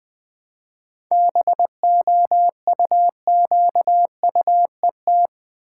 Morse code, heard as a keyed tone, about 21 wpm, 700 Hz. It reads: BOUQUET
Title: Difficult Words with Hints - 20wpm